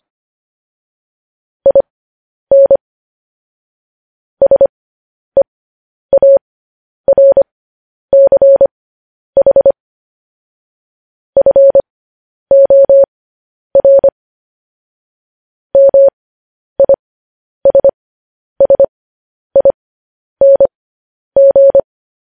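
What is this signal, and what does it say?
Morse code, heard as a keyed tone, about 25 wpm, spaced as for 10 wpm, 550 Hz. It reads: IN SEARCH FOR MISSING
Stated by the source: Morse code news headlines broadcast, opening with vvv.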